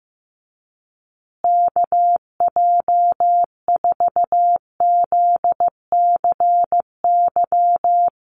Morse code, heard as a keyed tone, about 15 wpm, 700 Hz. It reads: KJ4ZCY
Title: USA Call Signs Repeated 2x 15wpm